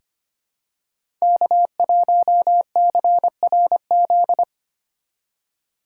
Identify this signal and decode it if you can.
Morse code, heard as a keyed tone, about 25 wpm, 700 Hz. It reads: K1CRZ
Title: USA Call Signs - 5 Characters - 8x Character Spacing - 25wpm